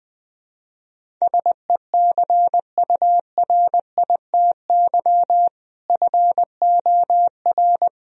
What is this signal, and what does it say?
Morse code, heard as a keyed tone, about 21 wpm, 700 Hz. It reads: SECURITY FOR
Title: Sentences from Top 1000 Words - Review Part 4 - 20wpm